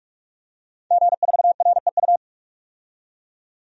Morse code, heard as a keyed tone, about 46 wpm, 700 Hz. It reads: G4REU
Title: DXCC-100 Call Signs - 5 Characters - 45wpm